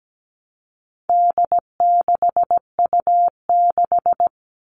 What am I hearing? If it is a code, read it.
Morse code, heard as a keyed tone, about 17 wpm, 700 Hz. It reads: D6U6
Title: All Cut Numbers - 4-Digit 17wpm